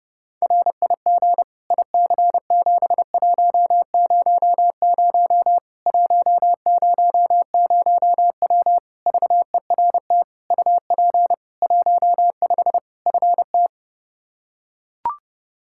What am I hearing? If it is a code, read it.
Morse code, heard as a keyed tone, about 32 wpm, 700 Hz. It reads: RIG IC7100 100W VERT UP 15 FT
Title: K1USN SST QSO Practice 30wpm